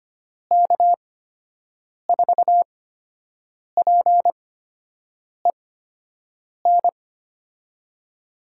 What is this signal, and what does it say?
Morse code, heard as a keyed tone, about 25 wpm, spaced as for 7 wpm, 700 Hz. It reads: K4PEN
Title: USA Call Signs - 6 Characters - 8x Character Spacing - 25wpm